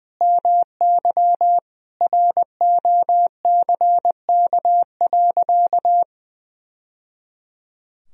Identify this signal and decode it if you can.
Morse code, heard as a keyed tone, about 20 wpm, 700 Hz. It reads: MY ROCK.
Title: Sentences from Top 1000 Words - Review Part 2 - 20wpm